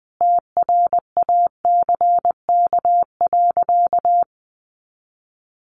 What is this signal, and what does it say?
Morse code, heard as a keyed tone, about 20 wpm, 700 Hz. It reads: TRACK.
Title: Sentences from Top 1100 Words 20wpm